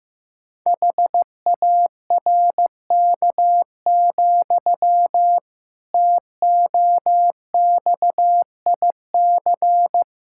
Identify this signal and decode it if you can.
Morse code, heard as a keyed tone, about 15 wpm, 700 Hz. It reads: HARK, TOXIC
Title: Pangram Challenge v2 - 15wpm